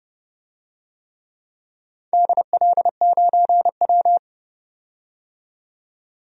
Morse code, heard as a keyed tone, about 30 wpm, 700 Hz. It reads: DL9W
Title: DXCC-100 Call Signs - 4 Characters - 30wpm